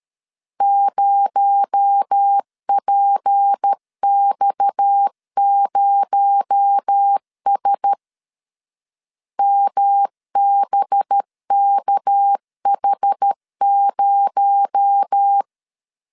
Morse code, heard as a keyed tone, about 13 wpm, 800 Hz. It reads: 0PX0S MBKH0
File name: CW / DET / 1 9 1500C S A 12wpm 3800